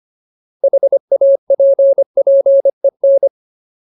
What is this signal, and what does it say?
Morse code, heard as a keyed tone, about 25 wpm, 550 Hz. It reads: HAPPEN
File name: Ultimate Word List / 25WPM Ultimate